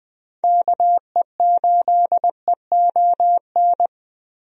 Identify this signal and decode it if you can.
Morse code, heard as a keyed tone, about 21 wpm, 700 Hz. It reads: KE8EON
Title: USA Call Signs - 6 Characters - Repeated 2x - 20wpm